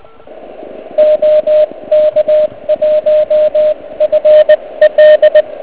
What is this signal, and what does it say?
Morse code, heard as a keyed tone, about 18 wpm, 610 Hz. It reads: OK1FL